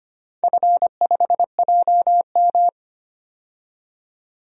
Morse code, heard as a keyed tone, about 25 wpm, 700 Hz. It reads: F5JM
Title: DXCC-100 Call Signs - 4 Characters - 25wpm